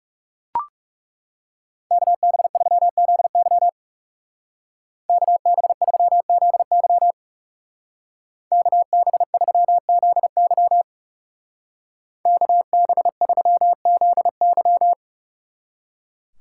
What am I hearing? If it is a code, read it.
Morse code, heard as a keyed tone, about 39 wpm, 700 Hz. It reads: KB3ZY KB3ZY KB3ZY KB3ZY